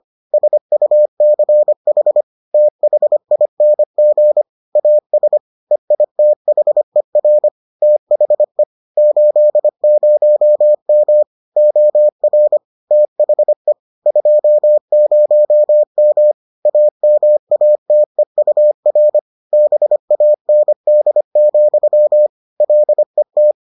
Morse code, heard as a keyed tone, about 25 wpm, 600 Hz. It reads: SUCH THING AS EITHER THE 80M OR THE 20M AMATEUR BAND, LET